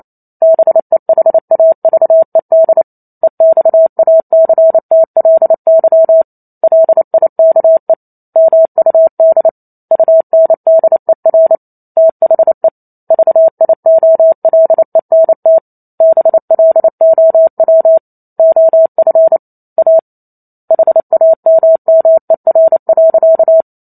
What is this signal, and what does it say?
Morse code, heard as a keyed tone, about 28 wpm, 650 Hz. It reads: BEHAVED EXACTLY LIKE MUD UNDER THE VIOLENT BLOW OF A HAMMER.